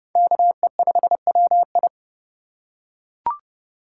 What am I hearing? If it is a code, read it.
Morse code, heard as a keyed tone, about 30 wpm, 700 Hz. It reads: KE5WI
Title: USA Call Signs - 5 Characters - 2x Character Spacing - 30wpm